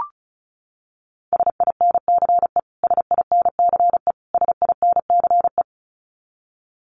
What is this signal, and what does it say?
Morse code, heard as a keyed tone, about 34 wpm, 700 Hz. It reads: SINCE SINCE SINCE
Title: Top 500 Words Repeated 35wpm